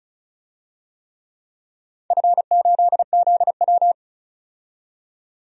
Morse code, heard as a keyed tone, about 34 wpm, 700 Hz. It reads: F8ZW
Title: DXCC-100 Call Signs - 4 Characters - 35wpm